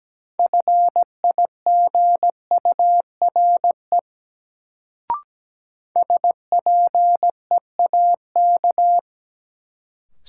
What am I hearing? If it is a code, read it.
Morse code, heard as a keyed tone, about 17 wpm, 700 Hz. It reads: FIGURE SPEAK
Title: Sentences from Top 1000 Words - Review Part 2 - New Word Drill - 17wpm